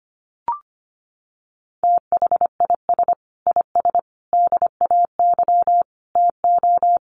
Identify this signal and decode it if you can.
Morse code, heard as a keyed tone, about 25 wpm, 700 Hz. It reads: THIS IS DAY TO
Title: Sets of 5 Words Encore 25wpm